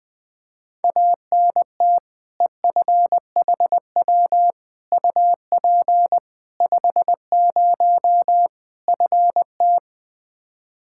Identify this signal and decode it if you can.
Morse code, heard as a keyed tone, about 20 wpm, 700 Hz. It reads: ANT EFHW UP 50 FT
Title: K1USN SST QSO Practice 20wpm